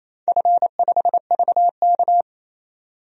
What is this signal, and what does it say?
Morse code, heard as a keyed tone, about 28 wpm, 700 Hz. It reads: F5VK
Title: DXCC-100 Call Signs - 4 Characters - 28wpm